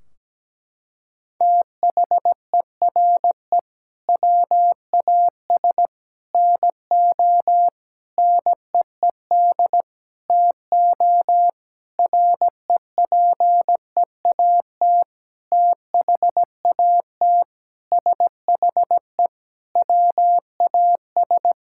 Morse code, heard as a keyed tone, about 17 wpm, 700 Hz. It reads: THERE WAS NO NEED TO REPEAT THAT SHE WAS
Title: Sentences from Top 1600 Words 17wpm